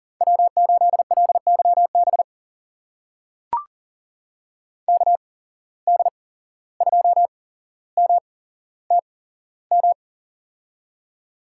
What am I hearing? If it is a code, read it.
Morse code, heard as a keyed tone, about 39 wpm, spaced as for 16 wpm, 700 Hz. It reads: W8LYB KD2MTM
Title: USA Call Signs - 8x Character Spacing - 40wpm